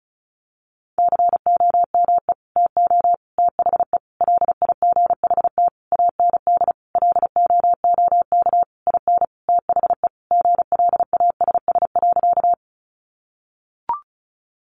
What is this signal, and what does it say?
Morse code, heard as a keyed tone, about 34 wpm, 700 Hz. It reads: COME TO THE LIGHT AND LOOK IN THE GLASS.
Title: Sentences from Top 1300 Words 35wpm